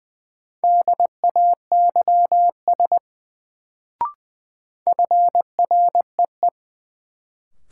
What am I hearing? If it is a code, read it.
Morse code, heard as a keyed tone, about 20 wpm, 700 Hz. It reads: DAYS FREE
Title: Sets of 5 Words Encore - New Word Drill - 20wpm